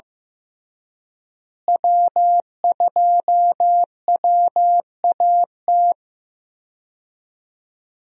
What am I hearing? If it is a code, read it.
Morse code, heard as a keyed tone, about 16 wpm, 700 Hz. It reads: W2WAT
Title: USA Call Signs - 5 Characters - Repeated 3x - 15wpm